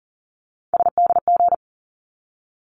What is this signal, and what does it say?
Morse code, heard as a keyed tone, about 40 wpm, 700 Hz. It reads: SDG
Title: ARRL Sections 40wpm